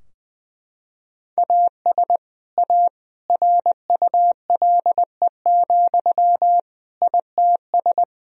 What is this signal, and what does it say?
Morse code, heard as a keyed tone, about 21 wpm, 700 Hz. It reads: AS A RULE, ITS
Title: Sentences from Top 1000 Words - Review Part 2 - 20wpm